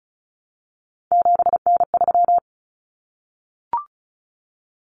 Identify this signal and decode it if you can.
Morse code, heard as a keyed tone, about 34 wpm, 700 Hz. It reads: 7N3
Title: Common Cut Numbers - 3-Digit 35wpm